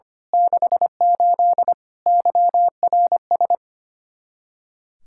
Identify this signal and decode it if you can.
Morse code, heard as a keyed tone, about 24 wpm, 700 Hz. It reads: 68 YRS
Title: Speed-Racing Common QSO Phrases - 25wpm